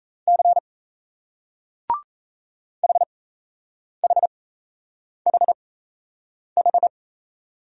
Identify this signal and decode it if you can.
Morse code, heard as a keyed tone, about 38 wpm, spaced as for 8 wpm, 700 Hz. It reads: C HHHH